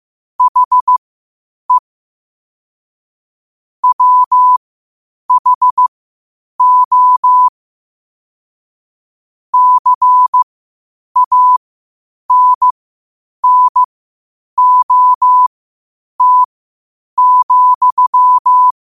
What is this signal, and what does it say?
Morse code, heard as a keyed tone, about 15 wpm, spaced as for 8 wpm, 1000 Hz. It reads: HE WHO CANNOT,
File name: qotdfiles / 8wpm